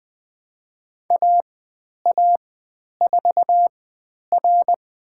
Morse code, heard as a keyed tone, about 20 wpm, spaced as for 10 wpm, 700 Hz. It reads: AA4R
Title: 095 Lesson 16g - Letter c - Call Signs 20wpm